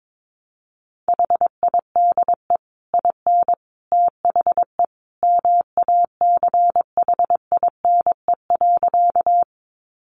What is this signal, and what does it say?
Morse code, heard as a keyed tone, about 22 wpm, 700 Hz. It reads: HIDE IN THE MACHINE.
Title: A Sound Of Thunder 22wpm